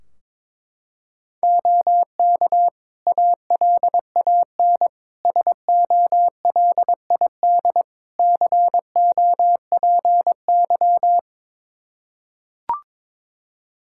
Morse code, heard as a keyed tone, about 22 wpm, 700 Hz. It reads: OK ALAN SOLID COPY
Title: Practice QSO - Exchange 3 - More Info - 22wpm